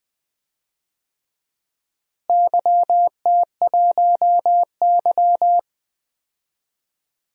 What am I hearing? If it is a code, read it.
Morse code, heard as a keyed tone, about 20 wpm, 700 Hz. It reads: YT1Y
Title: DXCC-100 Call Signs - 4 Characters - Repeated 2x - 20wpm